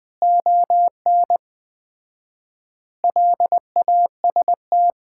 Morse code, heard as a keyed tone, about 20 wpm, 700 Hz. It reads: ON LAST
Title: Common QSO Phrases - 4x Word-Spacing - 20wpm